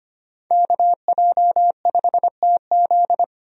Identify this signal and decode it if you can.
Morse code, heard as a keyed tone, about 25 wpm, 700 Hz. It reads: KJ5TZ
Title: USA Call Signs - 5 Characters - 25wpm